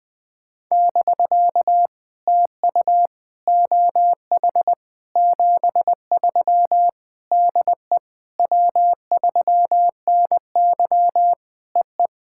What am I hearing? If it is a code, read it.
Morse code, heard as a keyed tone, about 20 wpm, 700 Hz. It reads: <BK> TU OH 73 DE W3NY EE
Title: POTA QSO Practice - 20wpm